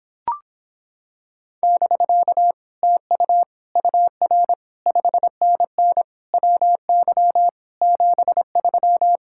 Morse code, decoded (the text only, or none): <BK> TU UR 5NN WY 73